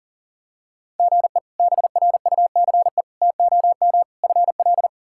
Morse code, heard as a keyed tone, about 37 wpm, 700 Hz. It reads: GE BRUCE TOM FL
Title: K1USN SST QSO Practice - 40wpm